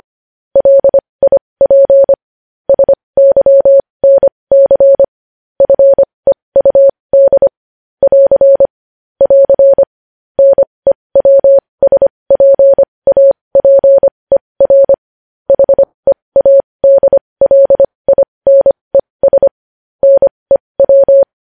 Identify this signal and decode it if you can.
Morse code, heard as a keyed tone, about 25 wpm, spaced as for 20 wpm, 550 Hz. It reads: LIP SYNC FEUD <AR> <AR> NEWSPAPER HEADLINES NEW
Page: news headlines in morse code at 20 wpm